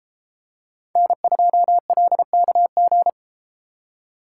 Morse code, heard as a keyed tone, about 33 wpm, 700 Hz. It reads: N2LKG